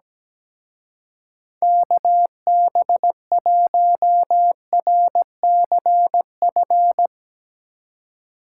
Morse code, heard as a keyed tone, about 18 wpm, 700 Hz. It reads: KB1RCF
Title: Speed Racing - USA 2x3 Call Signs 17wpm